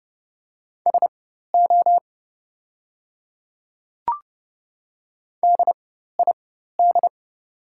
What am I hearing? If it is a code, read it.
Morse code, heard as a keyed tone, about 30 wpm, spaced as for 14 wpm, 700 Hz. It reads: SO DID
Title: 4x Character Spacing - Top 300 Words - 30wpm